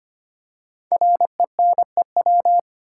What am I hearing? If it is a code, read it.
Morse code, heard as a keyed tone, about 25 wpm, 700 Hz. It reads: RENEW
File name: 5 Letter Words - First 3 Letters Given - 25wpm